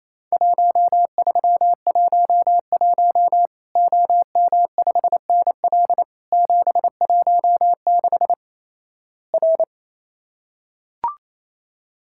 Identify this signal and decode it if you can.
Morse code, heard as a keyed tone, about 28 wpm, 700 Hz. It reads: 1311 OM5NL 716 R
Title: Worked All Europe DX Contest - QTC Exchange - 28wpm